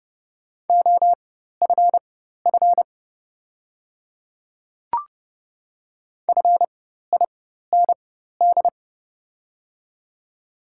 Morse code, heard as a keyed tone, about 30 wpm, spaced as for 14 wpm, 700 Hz. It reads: OFF FIND
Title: 4x Character Spacing - Top 200 Words - 30wpm